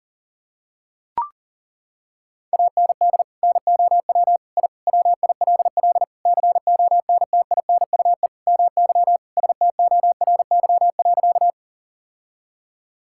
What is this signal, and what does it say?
Morse code, heard as a keyed tone, about 40 wpm, 700 Hz. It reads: AND NOW I WILL CONTINUE MY STORY.
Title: Sentences from Top 1000 Words - Review Part 6 - 40wpm